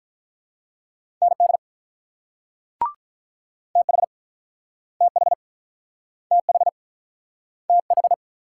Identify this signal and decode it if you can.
Morse code, heard as a keyed tone, about 48 wpm, 700 Hz. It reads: ND TH TH TH TH